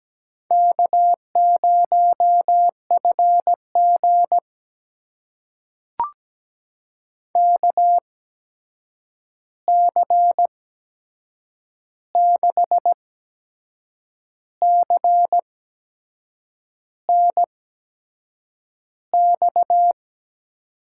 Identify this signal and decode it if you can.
Morse code, heard as a keyed tone, about 17 wpm, spaced as for 6 wpm, 700 Hz. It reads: K0FG KC6CNX
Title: USA Call Signs - 8x Character Spacing - 17wpm